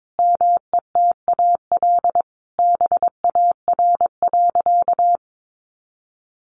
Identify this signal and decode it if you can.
Morse code, heard as a keyed tone, about 22 wpm, 700 Hz. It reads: METAL BAR.
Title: Additional Easy Phrases 22wpm